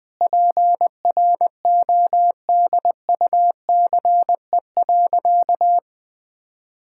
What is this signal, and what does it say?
Morse code, heard as a keyed tone, about 21 wpm, 700 Hz. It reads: PRODUCE.
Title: Sentences from Top 1000 Words - Review Part 5 - 20wpm